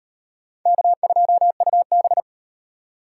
Morse code, heard as a keyed tone, about 36 wpm, 700 Hz. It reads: K2UB